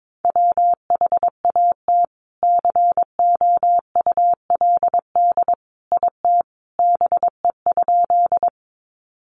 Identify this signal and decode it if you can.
Morse code, heard as a keyed tone, about 22 wpm, 700 Hz. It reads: WHAT COULD IT BE?
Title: Sentences from Top 100 Words 22wpm